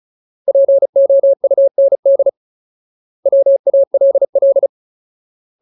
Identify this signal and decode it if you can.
Morse code, heard as a keyed tone, about 34 wpm, 550 Hz. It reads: POUND WALL
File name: Ultimate Word List / 35WPM Ultimate